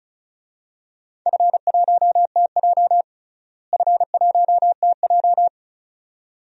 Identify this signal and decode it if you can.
Morse code, heard as a keyed tone, about 34 wpm, 700 Hz. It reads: F1TJ F1TJ
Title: DXCC-100 Call Signs - 4 Characters - Repeated 2x - 35wpm